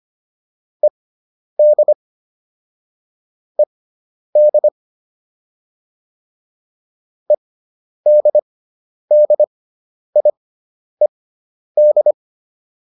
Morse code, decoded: ED ED EDDIED

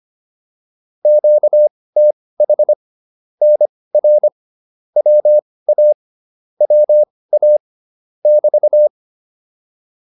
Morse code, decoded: QTH NR WA WA <BT>

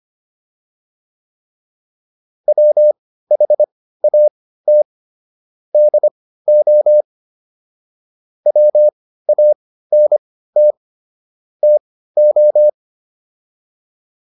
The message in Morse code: WHAT DO WANT TO

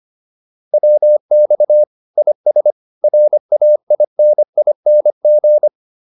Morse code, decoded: WX IS RAINING

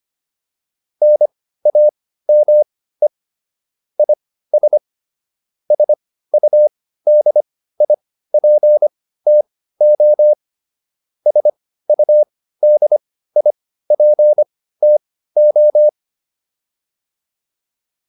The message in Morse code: NAME IS SUDIPTO SUDIPTO

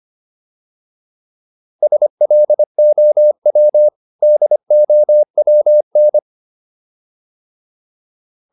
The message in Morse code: SLOW DOWN

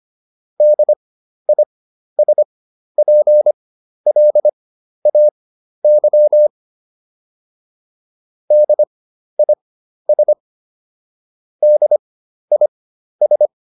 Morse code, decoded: DISPLAY DIS DIS

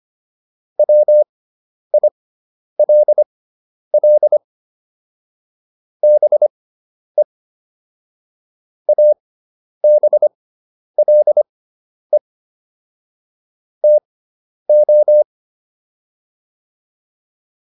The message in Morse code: WILL BE ABLE TO

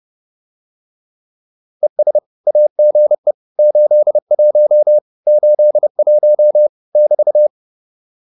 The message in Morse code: ES AGE 81 81 <BT>

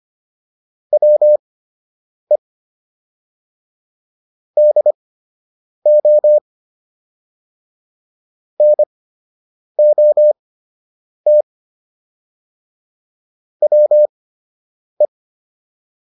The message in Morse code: WE DO NOT WE

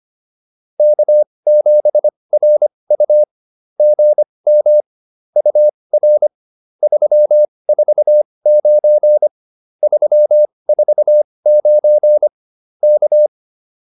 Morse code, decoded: K7RU GM UR 349 349 K

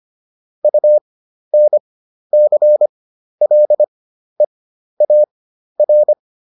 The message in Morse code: UNCLEAR

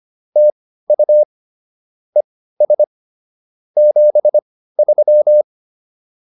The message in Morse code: TU ES 73

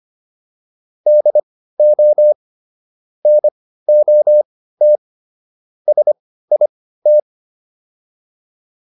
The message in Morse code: DO NOT SIT

